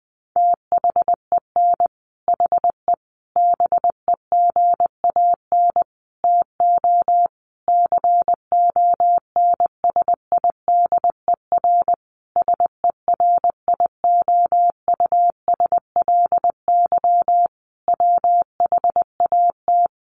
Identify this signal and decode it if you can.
Morse code, heard as a keyed tone, about 20 wpm, 700 Hz. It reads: THEN HE BEGAN TO CONSIDER SERIOUSLY WHAT
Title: Sentences from Top 1200 Words 20wpm